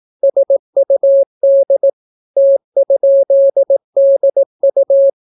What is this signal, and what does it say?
Morse code, heard as a keyed tone, about 18 wpm, 550 Hz. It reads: SUD T?DU